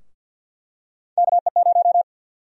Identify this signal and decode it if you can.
Morse code, heard as a keyed tone, about 53 wpm, 700 Hz. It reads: KE0
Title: USA Call Sign Prefixes - 50wpm